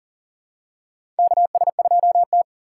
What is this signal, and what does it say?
Morse code, heard as a keyed tone, about 40 wpm, 700 Hz. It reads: KS2T
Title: USA Call Signs - 8x Character Spacing - 40wpm